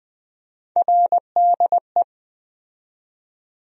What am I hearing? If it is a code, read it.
Morse code, heard as a keyed tone, about 21 wpm, 700 Hz. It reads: RDE
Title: USA Call Sign Postfixes - 3 Letters 20wpm